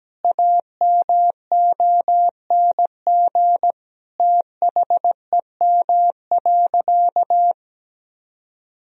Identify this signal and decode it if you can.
Morse code, heard as a keyed tone, about 17 wpm, 700 Hz. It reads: AMONG THEM.
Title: Sentences from Top 1000 Words - Review Part 2 - 17wpm